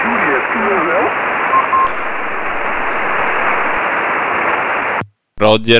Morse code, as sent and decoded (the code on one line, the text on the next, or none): none